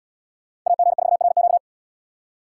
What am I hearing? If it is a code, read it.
Morse code, heard as a keyed tone, about 69 wpm, 700 Hz. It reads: AG4RQ